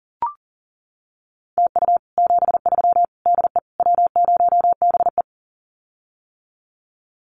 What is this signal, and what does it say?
Morse code, heard as a keyed tone, about 38 wpm, 700 Hz. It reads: TU 73 DE W0BE
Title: Common QSO Phrases Encore 40wpm